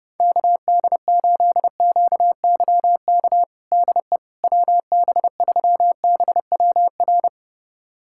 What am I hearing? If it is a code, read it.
Morse code, heard as a keyed tone, about 30 wpm, 700 Hz. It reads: KD8QYK DE WB3BWR
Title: USA Callsign DE Callsign 30wpm